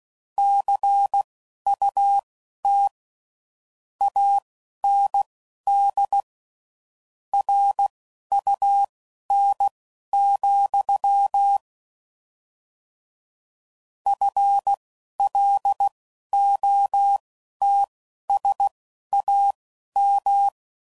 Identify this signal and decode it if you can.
Morse code, heard as a keyed tone, about 16 wpm, spaced as for 12 wpm, 800 Hz. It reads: CUT AND RUN, FLOTSAM
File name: Mixed / Misc / WordPairs